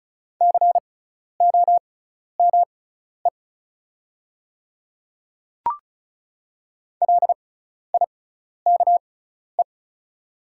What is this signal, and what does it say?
Morse code, heard as a keyed tone, about 36 wpm, spaced as for 12 wpm, 700 Hz. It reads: COME LIKE
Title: 6x Character Spacing - Top 100 Words - 35wpm